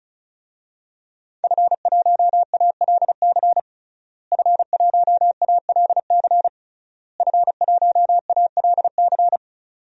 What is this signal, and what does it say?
Morse code, heard as a keyed tone, about 36 wpm, 700 Hz. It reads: F1ALC F1ALC F1ALC
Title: DXCC-100 Call Signs - 5 Characters - Repeated 3x - 35wpm